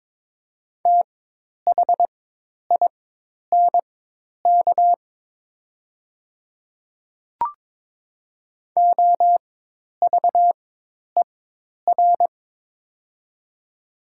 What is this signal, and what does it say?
Morse code, heard as a keyed tone, about 22 wpm, spaced as for 10 wpm, 700 Hz. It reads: THINK OVER
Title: 4x Character Spacing - Top 200 Words - 22wpm